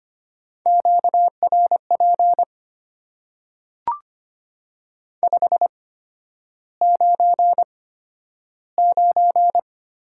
Morse code, heard as a keyed tone, about 25 wpm, spaced as for 11 wpm, 700 Hz. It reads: QRP 599
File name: Top 90 QSO Elements - 8x Character Spacing - 25wpm